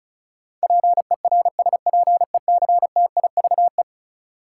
Morse code, heard as a keyed tone, about 36 wpm, 700 Hz. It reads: PERSPECTIVE
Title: Sentences from Top 1200 Words - New Word Drill Alt - 35wpm